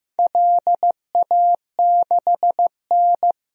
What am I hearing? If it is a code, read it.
Morse code, heard as a keyed tone, about 15 wpm, 700 Hz. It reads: LA6N
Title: DXCC-100 Call Signs - 4 Characters - Repeated 3x - 15wpm